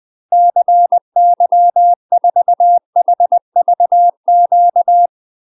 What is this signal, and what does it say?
Morse code, heard as a keyed tone, about 20 wpm, 700 Hz. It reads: CY4HVQ
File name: UK MASTER QSOs TRAINING / 20WPM / UK MASTER QSOs 0015